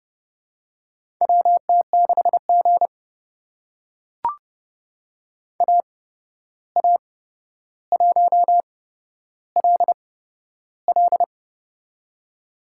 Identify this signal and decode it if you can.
Morse code, heard as a keyed tone, about 31 wpm, spaced as for 12 wpm, 700 Hz. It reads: WT6G AA1LL
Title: USA Call Signs - 8x Character Spacing - 30wpm